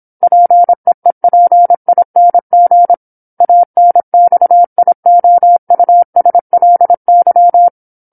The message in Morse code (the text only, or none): PEEPING ANXIOUSLY